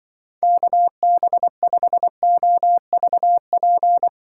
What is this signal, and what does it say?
Morse code, heard as a keyed tone, about 25 wpm, 700 Hz. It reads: KB5OVP
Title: Speed Racing - USA 2x3 Call Signs 20wpm